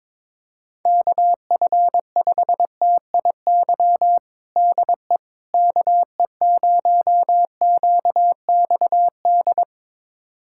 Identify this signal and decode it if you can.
Morse code, heard as a keyed tone, about 22 wpm, 700 Hz. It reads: KF5TIY DE KE0QXD
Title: USA Callsign DE Callsign - 6 Characters - 22wpm